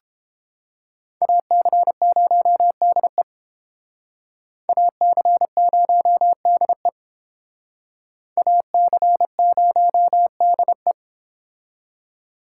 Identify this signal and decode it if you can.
Morse code, heard as a keyed tone, about 30 wpm, 700 Hz. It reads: AC0DE AC0DE AC0DE